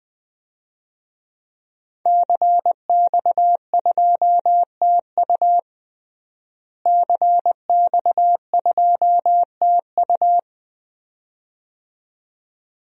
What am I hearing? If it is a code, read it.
Morse code, heard as a keyed tone, about 20 wpm, 700 Hz. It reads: CX2TU CX2TU
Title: DXCC-100 Call Signs - 5 Characters - Repeated 2x - 20wpm